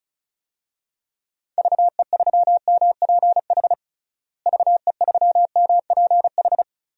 Morse code, heard as a keyed tone, about 36 wpm, 700 Hz. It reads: VE3MPH VE3MPH
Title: DXCC-100 Call Signs - 6 Characters - Repeated 3x - 35wpm